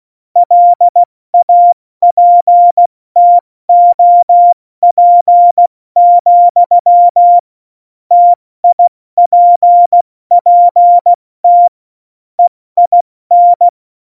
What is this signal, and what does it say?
Morse code, heard as a keyed tone, about 16 wpm, 700 Hz. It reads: LAPTOP, TIPPT EIN
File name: Morsekurs / Witze / Witz 26. Januar 2021, 70 F80